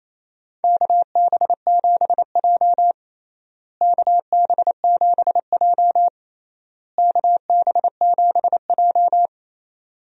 Morse code, heard as a keyed tone, about 27 wpm, 700 Hz. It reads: KB7J KB7J KB7J
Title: USA Call Signs - 4 Characters - Repeated 3x - 28wpm